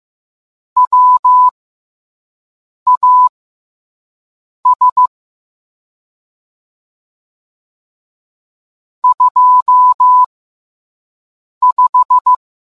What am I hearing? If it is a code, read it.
Morse code, heard as a keyed tone, about 15 wpm, spaced as for 5 wpm, 1000 Hz. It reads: WAS 25